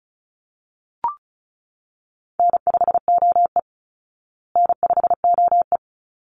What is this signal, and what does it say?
Morse code, heard as a keyed tone, about 34 wpm, 700 Hz. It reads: N5OE N5OE
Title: Easy Call Signs Repeated 2x - 35wpm